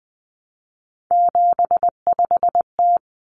Morse code, heard as a keyed tone, about 20 wpm, 700 Hz. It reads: 75T
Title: Common Cut Numbers - 3-Digit 20wpm